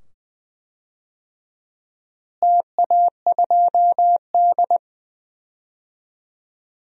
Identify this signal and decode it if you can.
Morse code, heard as a keyed tone, about 20 wpm, 700 Hz. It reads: TA2D
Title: DXCC-100 Call Signs - 4 Characters - Repeated 2x - 20wpm